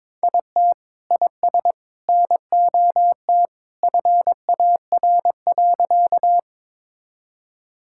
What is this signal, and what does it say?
Morse code, heard as a keyed tone, about 21 wpm, 700 Hz. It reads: IT IS NOT FAR.
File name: practice / 22wpm / Sentences with 3-Word Phrases - 22wpm